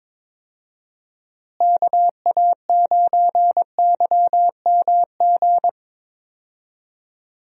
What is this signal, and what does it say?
Morse code, heard as a keyed tone, about 21 wpm, 700 Hz. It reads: KA9YMG
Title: USA Call Signs - 6 Characters - Repeated 3x - 22wpm